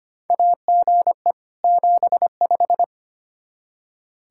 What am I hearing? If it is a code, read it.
Morse code, heard as a keyed tone, about 25 wpm, 700 Hz. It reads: AGE 75
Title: K1USN SST QSO Practice 25wpm